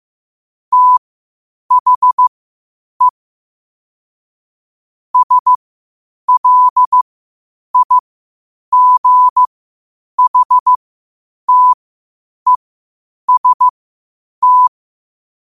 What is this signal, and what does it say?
Morse code, heard as a keyed tone, about 15 wpm, spaced as for 8 wpm, 1000 Hz. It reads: THE SLIGHTEST
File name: qotdfiles / 8wpm